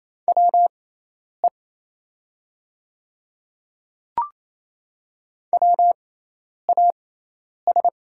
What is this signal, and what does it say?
Morse code, heard as a keyed tone, about 28 wpm, spaced as for 10 wpm, 700 Hz. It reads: WE WAS
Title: 6x Character Spacing - Top 100 Words - 28wpm